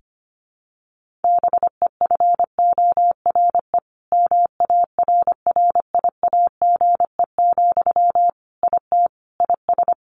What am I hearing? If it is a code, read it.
Morse code, heard as a keyed tone, about 25 wpm, 700 Hz. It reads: BEFORE MARRIAGE, IT IS
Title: Sentences from Top 1500 Words 25wpm